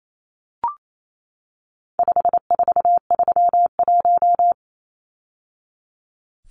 Morse code, heard as a keyed tone, about 28 wpm, 700 Hz. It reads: 5431
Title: Numbers - 4-Digit 28wpm